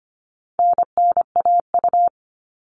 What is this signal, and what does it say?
Morse code, heard as a keyed tone, about 25 wpm, 700 Hz. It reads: NNAU
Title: All Cut Numbers - 4-Digit 25wpm